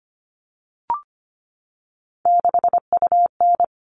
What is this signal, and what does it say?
Morse code, heard as a keyed tone, about 25 wpm, 700 Hz. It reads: 6UN
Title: All Cut Numbers - 3-Digit 25wpm